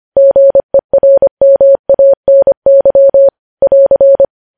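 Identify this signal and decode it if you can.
Morse code, heard as a keyed tone, about 25 wpm, 550 Hz. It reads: GERMANY <AR>
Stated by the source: Morse code news headlines at 25 wpm.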